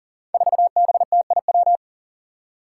Morse code, heard as a keyed tone, about 40 wpm, 700 Hz. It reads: 4BTIW
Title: Five Challenging Letter-Number Combinations 40wpm